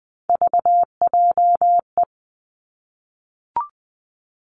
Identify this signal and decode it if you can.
Morse code, heard as a keyed tone, about 19 wpm, 700 Hz. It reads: VJE
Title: Three Challenging Letter-Number Combinations 20wpm